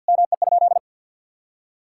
Morse code, heard as a keyed tone, about 50 wpm, 700 Hz. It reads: ME?
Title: Sentences from Top 1600 Words 50wpm